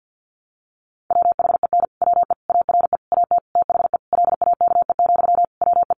Morse code, heard as a keyed tone, about 50 wpm, 700 Hz. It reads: WHEN WE ARE AT THE LAKE, WE
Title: Sentences from Top 1100 Words 50wpm